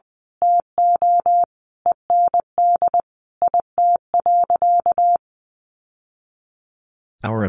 Morse code, heard as a keyed tone, about 20 wpm, 700 Hz. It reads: TO END IT.
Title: Sentences from Top 1200 Words 20wpm